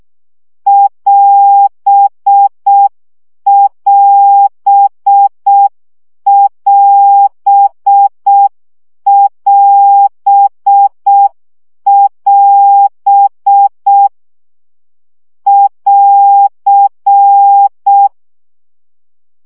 Morse code, heard as a keyed tone, about 6 wpm, 800 Hz. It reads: <AS><AS><AS><AS><AS> <AR>
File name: CW gyakorlas as 5 baud